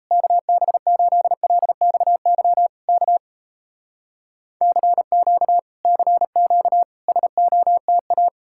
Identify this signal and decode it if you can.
Morse code, heard as a keyed tone, about 39 wpm, 700 Hz. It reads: KB8LXY K CQ CQ SOTA